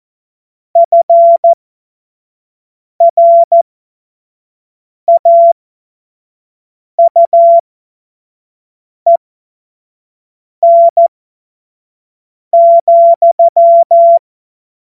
Morse code, heard as a keyed tone, about 14 wpm, spaced as for 5 wpm, 680 Hz. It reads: FRAUEN,
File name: Morsekurs / Witze / Witz 27. April 2021, 25 F70